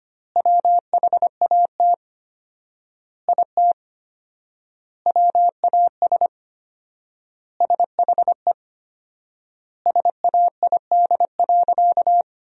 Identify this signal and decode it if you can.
Morse code, heard as a keyed tone, about 25 wpm, 700 Hz. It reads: WHAT IT WAS SHE SAID.
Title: 4x Word-Spacing - Sentences from Top 400 Words - 25wpm